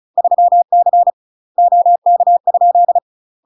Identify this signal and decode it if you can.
Morse code, heard as a keyed tone, about 36 wpm, 700 Hz. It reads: 3C OK?
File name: UK MASTER QSOs TRAINING / 35WPM / UK MASTER QSOs 0033